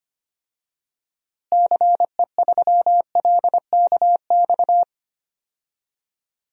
Morse code, heard as a keyed tone, about 25 wpm, 700 Hz. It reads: CE3LKX
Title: DXCC-100 Call Signs - 6 Characters - Repeated 2x - 25wpm